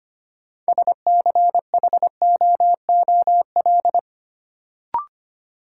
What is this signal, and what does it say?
Morse code, heard as a keyed tone, about 25 wpm, 700 Hz. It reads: SCHOOL
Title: Sets of 2 Words - New Word Drill - 25wpm